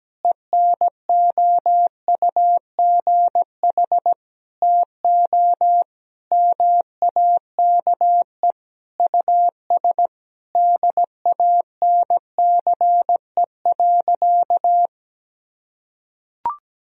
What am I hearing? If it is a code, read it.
Morse code, heard as a keyed tone, about 17 wpm, 700 Hz. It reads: ENOUGH TO MAKE US DANCE.
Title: Sentences from Top 1000 Words - Review Part 4 - 17wpm